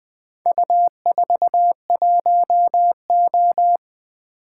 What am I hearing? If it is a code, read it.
Morse code, heard as a keyed tone, about 20 wpm, 700 Hz. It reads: U41O
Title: Four Letter-Number Combinations - 20wpm